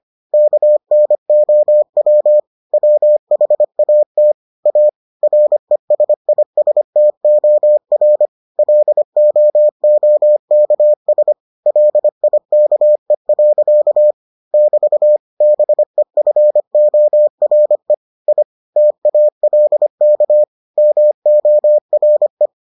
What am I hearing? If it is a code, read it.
Morse code, heard as a keyed tone, about 25 wpm, 600 Hz. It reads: KNOW WHAT A RESISTOR LOOKS LIKE. <BT> BEFORE I TALK MORE